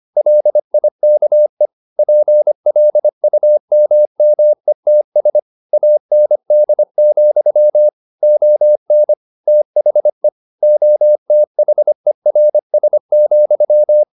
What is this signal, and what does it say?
Morse code, heard as a keyed tone, about 25 wpm, 600 Hz. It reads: LIKE PLUMMETS AND, ON THE OTHERS,
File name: the old man and the sea part 1 25wpm